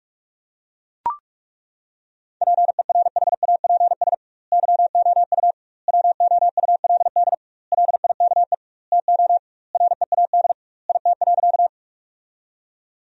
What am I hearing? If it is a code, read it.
Morse code, heard as a keyed tone, about 41 wpm, 700 Hz. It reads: PERHAPS YOU WOULD LIKE TO READ IT.
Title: 2x Word-Spacing - Sentences from Top 500 Words - 45wpm